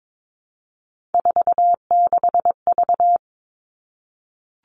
Morse code, heard as a keyed tone, about 22 wpm, 700 Hz. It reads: 46V
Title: All Cut Numbers - 3-Digit 22wpm